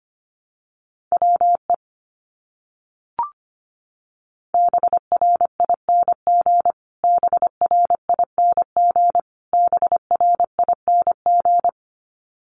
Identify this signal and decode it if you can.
Morse code, heard as a keyed tone, about 25 wpm, 700 Hz. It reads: WE BRING BRING BRING
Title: Top 500 Words Repeated 25wpm